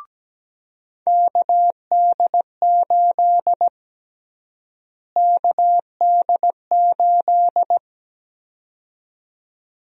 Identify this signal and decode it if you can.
Morse code, heard as a keyed tone, about 17 wpm, 700 Hz. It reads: KD8 KD8
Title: USA Call Sign Prefixes - Repeated 2x - 17wpm